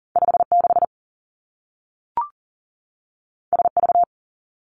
Morse code, heard as a keyed tone, about 42 wpm, 700 Hz. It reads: 5 6 S V
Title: Two Challenging Letter-Number Combinations 40wpm